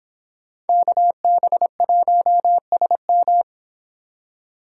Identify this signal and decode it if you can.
Morse code, heard as a keyed tone, about 26 wpm, 700 Hz. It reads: KB1SM